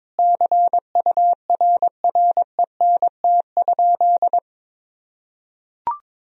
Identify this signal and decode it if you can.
Morse code, heard as a keyed tone, about 22 wpm, 700 Hz. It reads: CURRENT?
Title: Sentences from Top 1000 Words - Review Part 4 - 22wpm